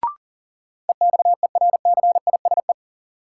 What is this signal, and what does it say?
Morse code, heard as a keyed tone, about 40 wpm, 700 Hz. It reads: EXERCISE
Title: Sentences from Top 1000 Words - Review Part 4 - New Word Drill - 40wpm